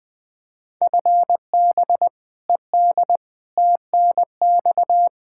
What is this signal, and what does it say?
Morse code, heard as a keyed tone, about 20 wpm, 700 Hz. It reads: FB ED TNX
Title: Practice QSO - Exchange 3 - More Info - 20wpm